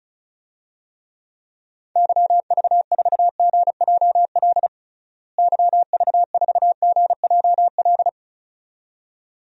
Morse code, heard as a keyed tone, about 34 wpm, 700 Hz. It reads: YV4GJL YV4GJL
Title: DXCC-100 Call Signs - 6 Characters - Repeated 2x - 35wpm